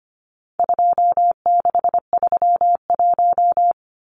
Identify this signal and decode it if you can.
Morse code, heard as a keyed tone, about 25 wpm, 700 Hz. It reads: 2631
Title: Numbers - 4-Digit 25wpm